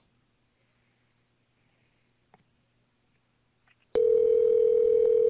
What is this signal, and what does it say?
Ringback tone (North America), 440 + 480 Hz.